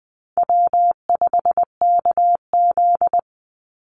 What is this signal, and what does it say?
Morse code, heard as a keyed tone, about 20 wpm, 700 Hz. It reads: W5KZ
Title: USA Call Signs 20wpm